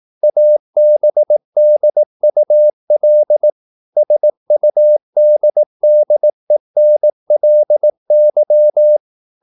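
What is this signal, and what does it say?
Morse code, heard as a keyed tone, about 18 wpm, 600 Hz. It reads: ABDUL SUDDENLY
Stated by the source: Morse practice at 18 wpm, English text, from CW med Gnister.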